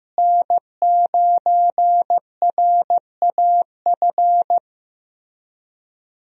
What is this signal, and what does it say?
Morse code, heard as a keyed tone, about 15 wpm, 700 Hz. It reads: N9RAF
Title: USA Call Signs - 5 Characters - Repeated 2x - 15wpm